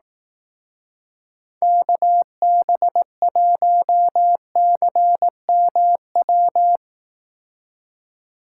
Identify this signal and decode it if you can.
Morse code, heard as a keyed tone, about 18 wpm, 700 Hz. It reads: KB1CMW
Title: Speed Racing - USA Call Signs 15wpm